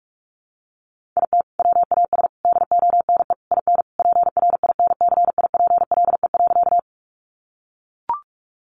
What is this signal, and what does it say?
Morse code, heard as a keyed tone, about 44 wpm, 700 Hz. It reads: IT WAS DONE IN PRINCIPLE.
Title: Sentences from Top 1400 Words 45wpm